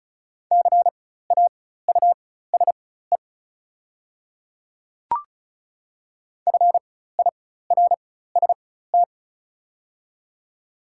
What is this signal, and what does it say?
Morse code, heard as a keyed tone, about 34 wpm, spaced as for 16 wpm, 690 Hz. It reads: CAUSE FIRST
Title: 4x Character Spacing - Top 200 Words - 35wpm